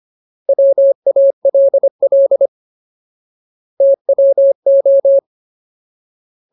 Morse code, heard as a keyed tone, about 25 wpm, 550 Hz. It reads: WALL TWO
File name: Ultimate Word List / 25WPM Ultimate